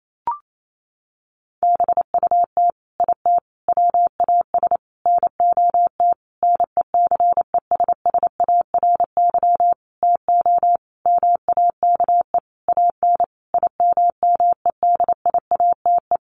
Morse code, heard as a keyed tone, about 28 wpm, 700 Hz. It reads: BUT IT WAS NOT NECESSARY TO MAKE AN IMMEDIATE
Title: Sentences from Top 1200 Words 28wpm